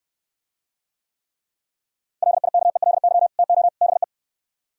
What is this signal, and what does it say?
Morse code, heard as a keyed tone, about 68 wpm, 700 Hz. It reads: LIKELY TO BE